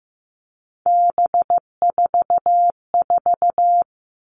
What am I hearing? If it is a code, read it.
Morse code, heard as a keyed tone, about 15 wpm, 700 Hz. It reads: B44
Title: All Cut Numbers - 3-Digit 15wpm